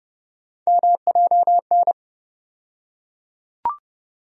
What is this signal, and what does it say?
Morse code, heard as a keyed tone, about 30 wpm, 700 Hz. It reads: MJN
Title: USA Call Sign Postfixes - 3 Letters 30wpm